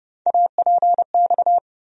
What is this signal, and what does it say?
Morse code, heard as a keyed tone, about 30 wpm, 700 Hz. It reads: APX
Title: USA Call Sign Postfixes - 3 Letters 30wpm